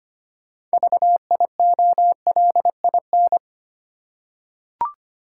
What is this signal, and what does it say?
Morse code, heard as a keyed tone, about 25 wpm, 690 Hz. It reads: VIOLIN
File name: practice / 25wpm / Easy Words with Hints - 25wpm